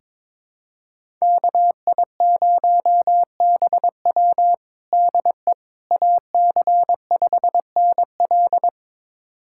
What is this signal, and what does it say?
Morse code, heard as a keyed tone, about 22 wpm, 700 Hz. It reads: KI0BW DE AC5NL
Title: USA Callsign DE Callsign - 5 Characters - 22wpm